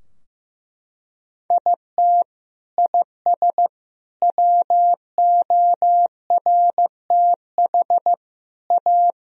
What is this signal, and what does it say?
Morse code, heard as a keyed tone, about 15 wpm, 700 Hz. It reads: IT IS WORTH A
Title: Sentences from Top 1000 Words - Review Part 3 - 15wpm